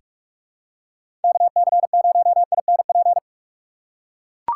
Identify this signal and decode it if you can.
Morse code, heard as a keyed tone, about 42 wpm, 700 Hz. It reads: KC0INP
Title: K1USN SST QSO Practice - 45wpm